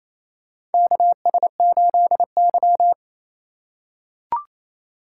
Morse code, heard as a keyed tone, about 28 wpm, 700 Hz. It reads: KS8Y
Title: USA Call Signs - 4 Characters - 4x Character Spacing - 28wpm